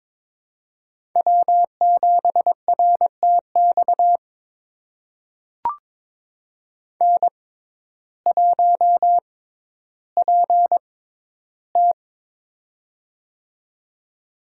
Morse code, heard as a keyed tone, about 22 wpm, spaced as for 12 wpm, 700 Hz. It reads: W7RTX N1PT
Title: USA Call Signs - 6x Character Spacing - 22wpm